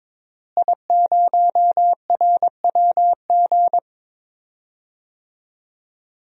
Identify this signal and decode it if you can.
Morse code, heard as a keyed tone, about 23 wpm, 700 Hz. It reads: I0RWG
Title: DXCC-100 Call Signs - 5 Characters - Repeated 2x - 22wpm